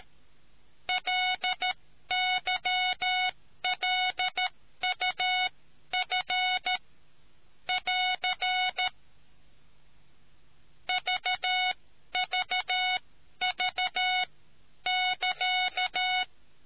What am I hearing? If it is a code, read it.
Morse code, heard as a keyed tone, about 13 wpm, spaced as for 11 wpm, 740 Hz. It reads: LYLUF <AR> VVV<KA>